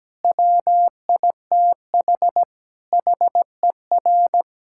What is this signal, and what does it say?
Morse code, heard as a keyed tone, about 17 wpm, 700 Hz. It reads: WITH HER
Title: Sentences from Top 1600 Words 17wpm